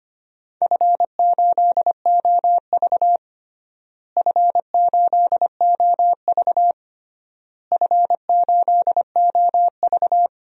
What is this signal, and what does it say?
Morse code, heard as a keyed tone, about 25 wpm, 700 Hz. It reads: F8OV F8OV F8OV
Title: DXCC-100 Call Signs - 4 Characters - Repeated 3x - 25wpm